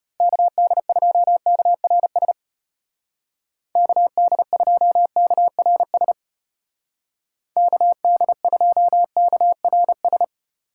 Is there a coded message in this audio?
Yes: Morse code from 0.2 to 10.2 s, heard as a keyed tone, about 34 wpm, 700 Hz. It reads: KD2KRS KD2KRS KD2KRS